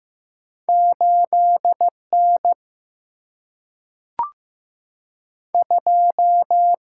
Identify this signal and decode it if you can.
Morse code, heard as a keyed tone, about 15 wpm, 700 Hz. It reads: 8 N 2
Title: Single-Letter Single-Number 15wpm